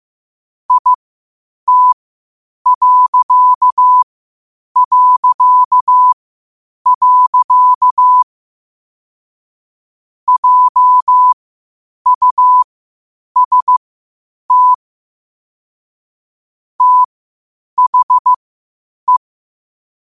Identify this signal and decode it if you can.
Morse code, heard as a keyed tone, about 15 wpm, spaced as for 8 wpm, 1000 Hz. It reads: IT... JUST THE
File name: qotdfiles / 8wpm